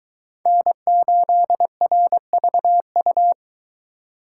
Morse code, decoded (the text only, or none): N8RVU